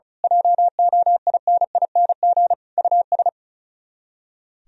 Morse code, heard as a keyed tone, about 33 wpm, 700 Hz. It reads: JOINING US
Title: Speed-Racing - Sets of 5 Words 35wpm